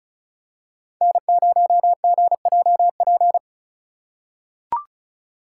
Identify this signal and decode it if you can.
Morse code, heard as a keyed tone, about 36 wpm, 700 Hz. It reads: N0GJP
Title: USA Call Signs - 5 Characters - 3x Character Spacing - 35wpm